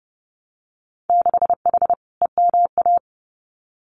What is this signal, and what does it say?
Morse code, heard as a keyed tone, about 30 wpm, 700 Hz. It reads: 6H EMA
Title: Winter Field Day Exchanges Encore 30wpm